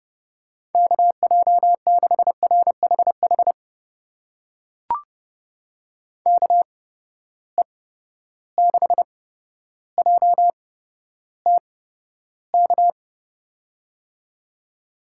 Morse code, heard as a keyed tone, about 30 wpm, spaced as for 13 wpm, 700 Hz. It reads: KJ6RHH KE6JTK
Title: USA Call Signs - 6 Characters - 8x Character Spacing - 30wpm